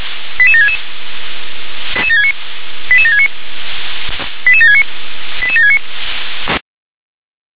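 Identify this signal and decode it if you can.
Five-tone selective call (ZVEI), heard as quick runs of five tones, tones of about 70 ms. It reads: 88766 88769 88769 88769
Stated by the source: ZVEI five-tone selective call received by SDR.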